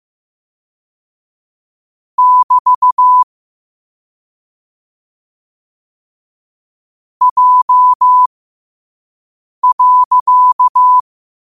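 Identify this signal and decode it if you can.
Morse code, heard as a keyed tone, about 15 wpm, spaced as for 5 wpm, 1000 Hz. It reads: <BT> J.
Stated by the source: Morse code practice at 5 wpm.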